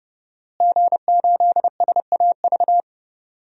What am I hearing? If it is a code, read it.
Morse code, heard as a keyed tone, about 31 wpm, 700 Hz. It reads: G8SAV
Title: DXCC-100 Call Signs - 5 Characters - Repeated 2x - 30wpm